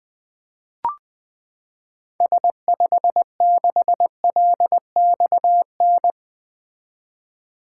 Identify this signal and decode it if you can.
Morse code, heard as a keyed tone, about 21 wpm, 700 Hz. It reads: S56LXN
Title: 2023 SOTA Activator Call Signs - 20wpm